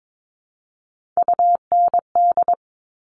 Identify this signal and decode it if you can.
Morse code, heard as a keyed tone, about 22 wpm, 700 Hz. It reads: UND
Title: Top Three Letter Combinations in Words 22wpm